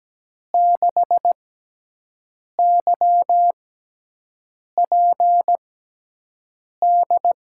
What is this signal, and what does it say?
Morse code, heard as a keyed tone, about 17 wpm, spaced as for 6 wpm, 700 Hz. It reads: 6YPD
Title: USA Call Signs - 6 Characters - 6x Character Spacing - 17wpm